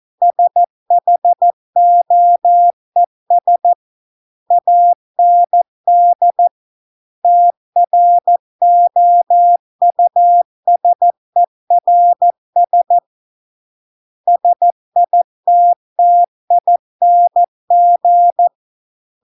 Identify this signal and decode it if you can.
Morse code, heard as a keyed tone, about 14 wpm, 700 Hz. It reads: SHOES AND TROUSERS SITTING